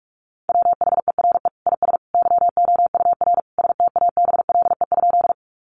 Morse code, heard as a keyed tone, about 44 wpm, 700 Hz. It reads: WHERE IS YOUR STABLE?
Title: Sentences from Top 1500 Words 45wpm